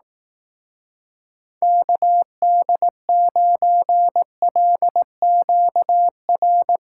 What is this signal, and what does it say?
Morse code, heard as a keyed tone, about 18 wpm, 700 Hz. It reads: KD9LQR
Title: Speed Racing - USA 2x3 Call Signs 15wpm